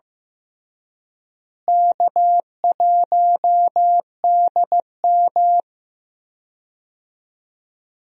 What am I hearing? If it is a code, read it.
Morse code, heard as a keyed tone, about 15 wpm, 700 Hz. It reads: K1DM
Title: USA Call Signs - 4 Characters - Repeated 2x - 15wpm